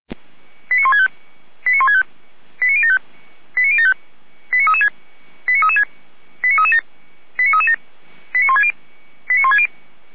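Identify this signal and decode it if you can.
Five-tone selective call (ZVEI), heard as quick runs of five tones, tones of about 70 ms. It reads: 89165 89165 89075 89075 89336 89337 89337 89160 89160
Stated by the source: ZVEI five-tone selective call received by SDR.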